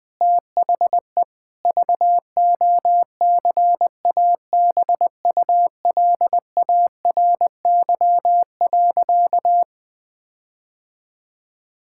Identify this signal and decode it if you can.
Morse code, heard as a keyed tone, about 20 wpm, 700 Hz. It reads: THE VOCABULARY.
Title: Difficult Sentences with Hints - 20wpm